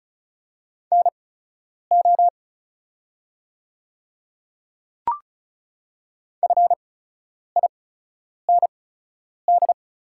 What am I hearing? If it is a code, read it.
Morse code, heard as a keyed tone, about 36 wpm, spaced as for 10 wpm, 700 Hz. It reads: NO FIND
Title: 8x Character Spacing - Top 100 Words - 35wpm